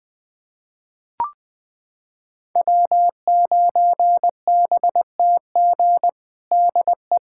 Morse code, heard as a keyed tone, about 20 wpm, 700 Hz. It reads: W9BTG DE
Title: USA Callsign DE Callsign - 5 Characters - 20wpm